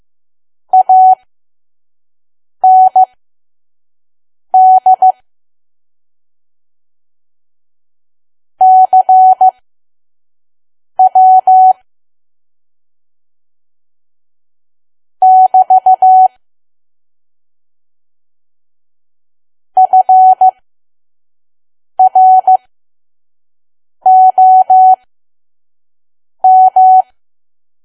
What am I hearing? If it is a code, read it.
Morse code, heard as a keyed tone, about 15 wpm, spaced as for 5 wpm, 750 Hz. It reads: AND CW <BT> FROM
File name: Archive / 5 WPM files / PM